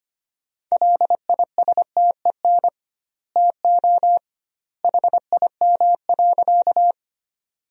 Morse code, heard as a keyed tone, about 25 wpm, 700 Hz. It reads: LISTEN TO HIM.
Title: 2x Word-Spacing - Sentences from Top 400 Words - 25wpm